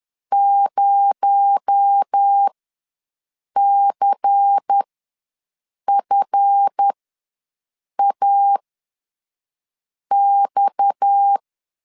Morse code, heard as a keyed tone, about 10 wpm, spaced as for 5 wpm, 800 Hz. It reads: 0CFAX
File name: LEZIONI / 7 GF9Z / 7 GF ws 10wpm 3918